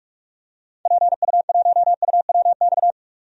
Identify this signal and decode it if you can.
Morse code, heard as a keyed tone, about 41 wpm, 700 Hz. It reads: PU1UWX
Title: DXCC-100 Call Signs - 6 Characters - 45wpm